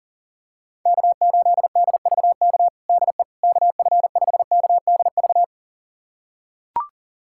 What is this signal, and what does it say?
Morse code, heard as a keyed tone, about 37 wpm, 700 Hz. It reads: K8DVK DE KF5KDV
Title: USA Callsign DE Callsign 40wpm